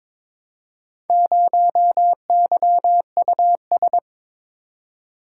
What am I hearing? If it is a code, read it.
Morse code, heard as a keyed tone, about 22 wpm, 700 Hz. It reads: 0YUS
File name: practice / 22wpm / Four Letter-Number Combinations - 22wpm